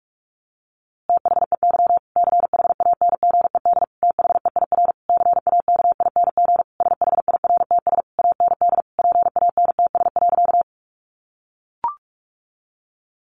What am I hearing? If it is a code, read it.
Morse code, heard as a keyed tone, about 44 wpm, 700 Hz. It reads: THEY CHANGED THEIR CAKING SHIRTS AND PANTS.
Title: A Sound Of Thunder 45wpm